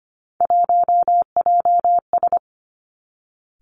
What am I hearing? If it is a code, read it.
Morse code, heard as a keyed tone, about 25 wpm, 700 Hz. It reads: 1JS
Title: Three Challenging Letter-Number Combinations 25wpm